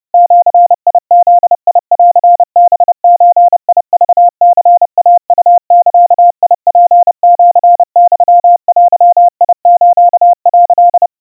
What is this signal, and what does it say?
Morse code (Japanese Wabun code), heard as a keyed tone, about 30 wpm, 700 Hz. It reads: ジブンハゾクニイウザツシユデス」
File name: 110 cpm 700 Hz Crispy